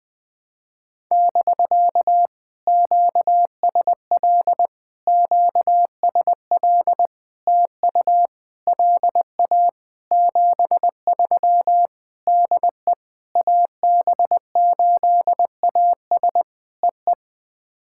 Morse code, heard as a keyed tone, about 20 wpm, 700 Hz. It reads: <BK> QSL QSL TU LA 73 DE AB8AS EE
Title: POTA QSO Practice - 20wpm